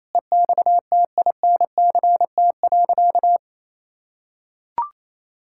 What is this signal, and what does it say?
Morse code, heard as a keyed tone, about 28 wpm, 700 Hz. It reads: EXTINCT.
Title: Pangram Challenge v2 - 28wpm